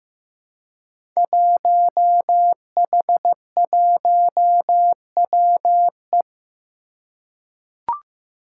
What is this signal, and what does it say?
Morse code, heard as a keyed tone, about 15 wpm, 700 Hz. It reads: 1H1WE
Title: Five Challenging Letter-Number Combinations 15wpm